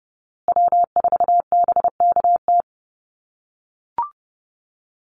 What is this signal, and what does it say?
Morse code, heard as a keyed tone, about 30 wpm, 700 Hz. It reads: W4BKT
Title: USA Call Signs Repeated 2x 30wpm